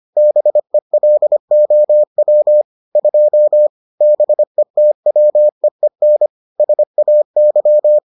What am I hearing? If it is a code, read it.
Morse code, heard as a keyed tone, about 25 wpm, 600 Hz. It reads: BELOW 2 BETWEEN SAY